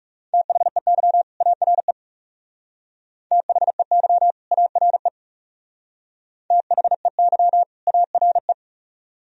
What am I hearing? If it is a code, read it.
Morse code, heard as a keyed tone, about 40 wpm, 700 Hz. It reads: THEY ARE THEY ARE THEY ARE